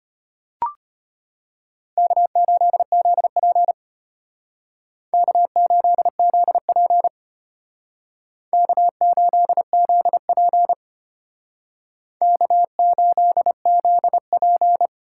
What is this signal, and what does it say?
Morse code, heard as a keyed tone, about 34 wpm, 700 Hz. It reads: K8ZP K8ZP K8ZP K8ZP